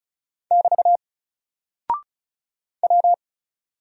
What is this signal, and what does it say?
Morse code, heard as a keyed tone, about 34 wpm, 700 Hz. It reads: <BT> W
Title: Single Character (Letter:Number:Punctuation:Prosign) Alt 35wpm